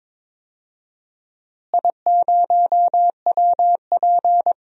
Morse code, heard as a keyed tone, about 21 wpm, 700 Hz. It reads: I0WP
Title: DXCC-100 Call Signs - 4 Characters - Repeated 3x - 22wpm